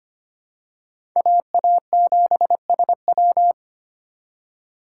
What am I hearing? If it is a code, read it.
Morse code, heard as a keyed tone, about 25 wpm, 690 Hz. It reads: AA7SW
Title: USA Call Signs - 4x Character Spacing - 25wpm